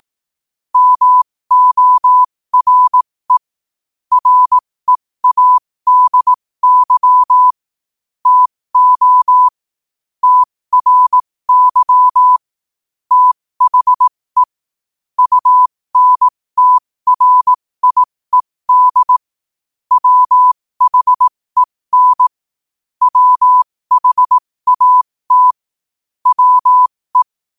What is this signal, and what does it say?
Morse code, heard as a keyed tone, about 18 wpm, spaced as for 15 wpm, 1000 Hz. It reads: MORE READY TO TRY THE UNTRIED WHEN WHAT WE
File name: qotdfiles / 15wpm